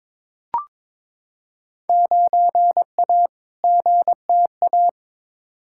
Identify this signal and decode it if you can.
Morse code, heard as a keyed tone, about 22 wpm, 700 Hz. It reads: 9A GTA
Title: Field Day Exchanges Encore - 22wpm